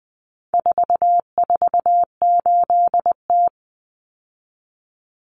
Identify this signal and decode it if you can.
Morse code, heard as a keyed tone, about 20 wpm, 700 Hz. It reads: 448T
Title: Common Cut Numbers - 4-Digit 20wpm